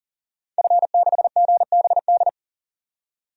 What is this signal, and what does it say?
Morse code, heard as a keyed tone, about 37 wpm, 700 Hz. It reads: F6GBD
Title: DXCC-100 Call Signs - 5 Characters - 40wpm